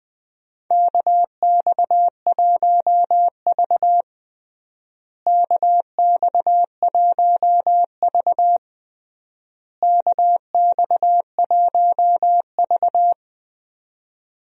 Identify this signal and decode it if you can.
Morse code, heard as a keyed tone, about 20 wpm, 700 Hz. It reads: KX1V KX1V KX1V
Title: USA Call Signs - 4 Characters - Repeated 3x - 20wpm